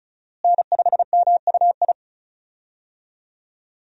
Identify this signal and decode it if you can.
Morse code, heard as a keyed tone, about 34 wpm, 700 Hz. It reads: N5MUI
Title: USA Call Signs - 5 Characters - Repeated 2x - 35wpm